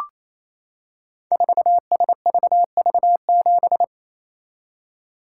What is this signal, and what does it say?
Morse code, heard as a keyed tone, about 27 wpm, 700 Hz. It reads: 4SVV7
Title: Code Groups - 5 Letter:Numbers - 28wpm